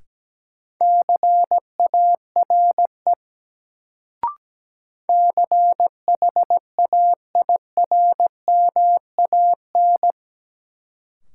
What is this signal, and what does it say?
Morse code, heard as a keyed tone, about 17 wpm, 700 Hz. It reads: CARE CHAIRMAN
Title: Sets of 3 Words - New Word Drill - 17wpm